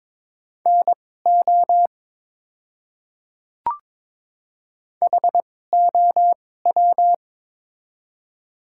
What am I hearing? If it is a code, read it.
Morse code, heard as a keyed tone, about 21 wpm, spaced as for 16 wpm, 690 Hz. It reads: NO HOW
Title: 2x Character Spacing - Top 100 Words - 22wpm